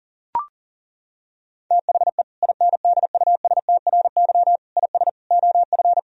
Morse code, heard as a keyed tone, about 37 wpm, 700 Hz. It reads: THE INDUSTRY IS OF
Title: Sentences from Top 1000 Words - Review Part 4 - 40wpm